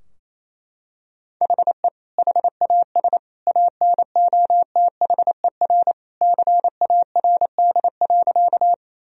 Morse code, heard as a keyed tone, about 28 wpm, 700 Hz. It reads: HE HAS ANOTHER CARD.
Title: Sentences from Top 1000 Words - Review Part 4 - 28wpm